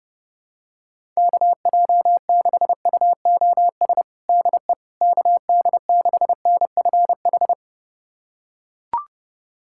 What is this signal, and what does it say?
Morse code, heard as a keyed tone, about 32 wpm, 700 Hz. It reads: KJ6UOS DE KD6NFH
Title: USA Callsign DE Callsign - 6 Characters - 30wpm